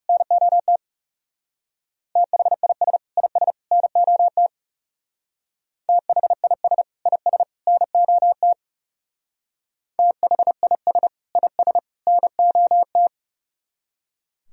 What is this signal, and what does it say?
Morse code, heard as a keyed tone, about 36 wpm, 700 Hz. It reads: NOT THIS IS NOT THIS IS NOT THIS IS NOT
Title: Speed-Racing - Sets of 3 Words 30wpm